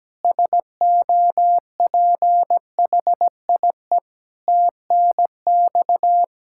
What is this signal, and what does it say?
Morse code, heard as a keyed tone, about 17 wpm, 700 Hz. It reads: SOPHIE TNX
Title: Practice QSO - Exchange 3 - More Info - 17wpm